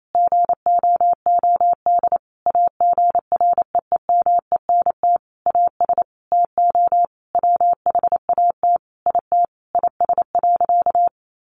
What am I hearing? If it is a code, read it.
Morse code, heard as a keyed tone, about 28 wpm, 700 Hz. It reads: GOOD AGREEMENT AS TO WHAT IT IS.
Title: Sentences from Top 1500 Words 28wpm